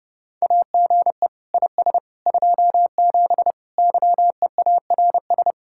Morse code, heard as a keyed tone, about 30 wpm, 700 Hz. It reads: AGE IS 27 YEARS